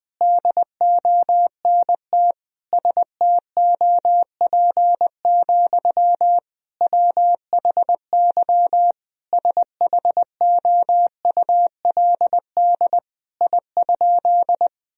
Morse code, heard as a keyed tone, about 21 wpm, 700 Hz. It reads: DONT STOP, WHY SHOULD I?
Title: Sentences from Top 1000 Words - Review Part 1 - 20wpm